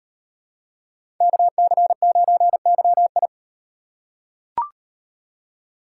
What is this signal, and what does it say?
Morse code, heard as a keyed tone, about 36 wpm, 700 Hz. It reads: KC9YI